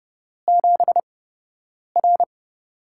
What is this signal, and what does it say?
Morse code, heard as a keyed tone, about 30 wpm, spaced as for 8 wpm, 700 Hz. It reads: 7R
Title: USA Call Signs - 4 Characters - 8x Character Spacing - 30wpm